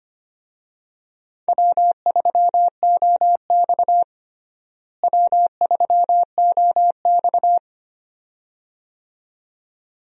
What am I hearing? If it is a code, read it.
Morse code, heard as a keyed tone, about 25 wpm, 700 Hz. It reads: W3OX W3OX
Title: USA Call Signs - 4 Characters - Repeated 2x - 25wpm